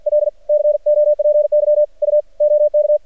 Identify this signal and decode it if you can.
Morse code, heard as a keyed tone, about 56 wpm, 600 Hz, with radio noise. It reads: L COPY U OK
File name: archived / CRSnet